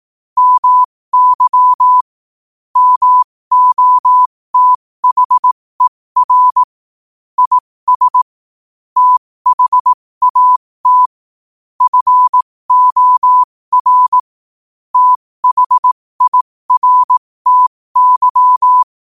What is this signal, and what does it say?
Morse code, heard as a keyed tone, about 18 wpm, spaced as for 15 wpm, 1000 Hz. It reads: MY MOTHER IS THAT FOR THIRTY
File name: 15wpm